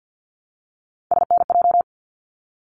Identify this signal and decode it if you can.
Morse code, heard as a keyed tone, about 50 wpm, 700 Hz. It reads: SNJ
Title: ARRL Sections 50wpm